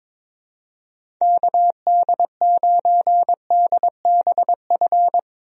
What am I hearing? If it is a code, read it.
Morse code, heard as a keyed tone, about 22 wpm, 700 Hz. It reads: KD9DBF
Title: USA Call Signs - 6 Characters - Repeated 2x - 22wpm